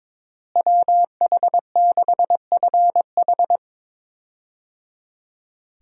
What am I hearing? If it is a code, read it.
Morse code, heard as a keyed tone, about 22 wpm, 700 Hz. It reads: WH6FH
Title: USA Call Signs - 5 Characters - Repeated 3x - 22wpm